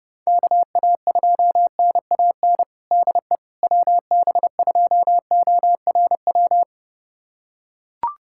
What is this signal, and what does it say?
Morse code, heard as a keyed tone, about 32 wpm, 700 Hz. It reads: KA2NAN DE WB2ORW
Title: USA Callsign DE Callsign 30wpm